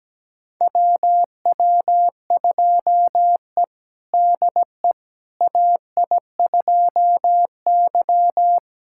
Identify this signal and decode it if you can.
Morse code, heard as a keyed tone, about 17 wpm, 700 Hz. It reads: WW2E DE AI2Y
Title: USA Callsign DE Callsign - 4 Characters - 17wpm